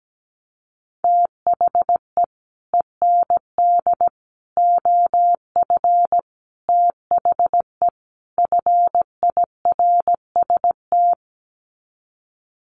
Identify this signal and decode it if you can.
Morse code, heard as a keyed tone, about 17 wpm, 700 Hz. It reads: THE END OF THE FIRST
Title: Sets of 5 Words Encore 17wpm